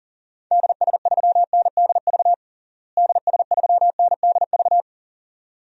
Morse code, heard as a keyed tone, about 40 wpm, 700 Hz. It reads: DS3NDV DS3NDV
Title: DXCC-100 Call Signs - 6 Characters - Repeated 2x - 40wpm